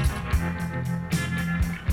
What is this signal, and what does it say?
There is no code.